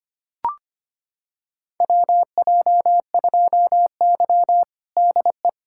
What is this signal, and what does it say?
Morse code, heard as a keyed tone, about 25 wpm, 700 Hz. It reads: WJ2Y DE
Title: USA Callsign DE Callsign - 4 Characters - 25wpm